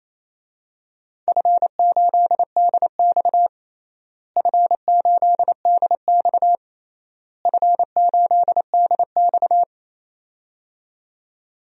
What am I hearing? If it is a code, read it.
Morse code, heard as a keyed tone, about 28 wpm, 700 Hz. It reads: F8DX F8DX F8DX
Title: DXCC-100 Call Signs - 4 Characters - Repeated 3x - 28wpm